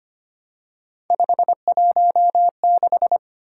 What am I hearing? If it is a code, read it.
Morse code, heard as a keyed tone, about 29 wpm, 700 Hz. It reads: 516
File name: Speed-Racing - Three Letter-Number Combinations - 25wpm